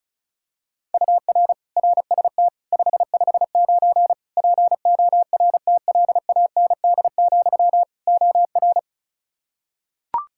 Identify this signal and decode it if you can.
Morse code, heard as a keyed tone, about 36 wpm, 700 Hz. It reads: UR RST 559 PORTLAND, OR
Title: K1USN SST QSO Practice 35wpm